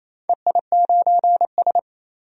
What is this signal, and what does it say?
Morse code, heard as a keyed tone, about 27 wpm, 700 Hz. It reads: EI9S
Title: DXCC-100 Call Signs - 4 Characters - 28wpm